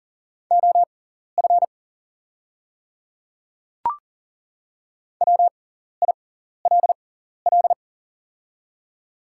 Morse code, heard as a keyed tone, about 39 wpm, spaced as for 14 wpm, 690 Hz. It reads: OF WILL
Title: 6x Character Spacing - Top 100 Words - 40wpm